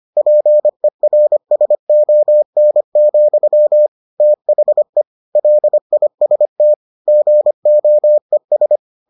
Morse code, heard as a keyed tone, about 25 wpm, 600 Hz. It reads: PERSON, THE LIST GOES